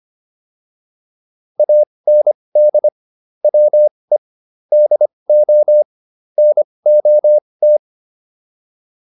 Morse code, heard as a keyed tone, about 25 wpm, spaced as for 20 wpm, 600 Hz. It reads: AND WE DO NOT